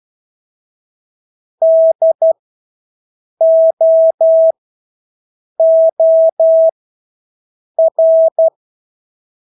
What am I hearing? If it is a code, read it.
Morse code, heard as a keyed tone, about 12 wpm, spaced as for 6 wpm, 650 Hz. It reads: DOOR
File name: Engelske ord 12-6 wpm